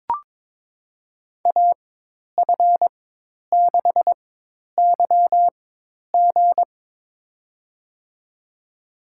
Morse code, heard as a keyed tone, about 22 wpm, spaced as for 10 wpm, 700 Hz. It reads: AF6YG
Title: USA Call Signs - 5 Characters - 4x Character Spacing - 22wpm